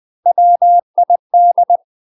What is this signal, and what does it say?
Morse code, heard as a keyed tone, about 20 wpm, 700 Hz. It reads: WID
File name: UK MASTER QSOs TRAINING / 20WPM / UK MASTER QSOs 0014